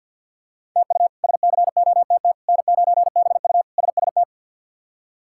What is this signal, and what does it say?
Morse code, heard as a keyed tone, about 52 wpm, 690 Hz. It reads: TU SCOTT N9BU SST
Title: K1USN SST QSO Practice - 50wpm